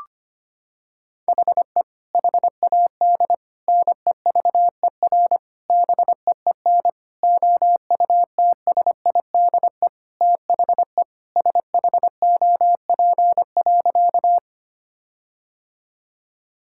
Morse code, heard as a keyed tone, about 25 wpm, 700 Hz. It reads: HE HAD NEVER BEEN OUTSIDE THE SHOP.
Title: Sentences from Top 1600 Words 25wpm